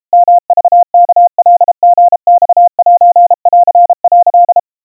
Morse code (Japanese Wabun code), heard as a keyed tone, about 32 wpm, 700 Hz. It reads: ヨクワカリマセン」
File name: kawa / Morse sample / kamyu 120 cpm 700 Hz Crispy